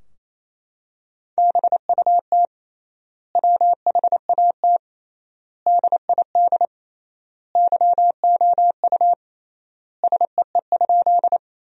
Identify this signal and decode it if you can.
Morse code, heard as a keyed tone, about 28 wpm, 700 Hz. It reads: BUT WHAT DID YOU SEE?
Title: 3x Word-Spacing - Sentences from Top 100 Words - 28wpm